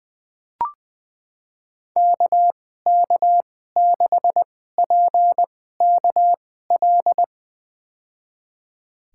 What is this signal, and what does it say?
Morse code, heard as a keyed tone, about 20 wpm, spaced as for 14 wpm, 700 Hz. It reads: KK6PKL
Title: USA Call Signs - 6 Characters - 2x Character Spacing - 20wpm